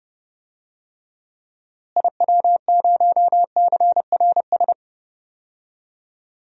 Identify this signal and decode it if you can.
Morse code, heard as a keyed tone, about 30 wpm, 700 Hz. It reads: IW0CRS
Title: DXCC-100 Call Signs - 6 Characters - Repeated 3x - 30wpm